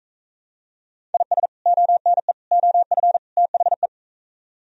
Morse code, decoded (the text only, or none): IS ONE OF THE